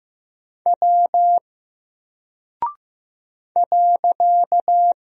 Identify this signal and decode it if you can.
Morse code, heard as a keyed tone, about 15 wpm, 700 Hz. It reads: W .
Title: Single Character (Letter:Number:Punctuation:Prosign) 15wpm